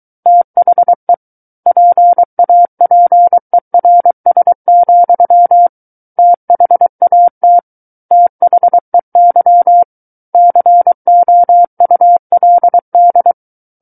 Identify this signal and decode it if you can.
Morse code, heard as a keyed tone, about 23 wpm, 700 Hz. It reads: THE PAPERS, THAT THEY COULD
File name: Book One / 14-Chapter 14 - 23 WPM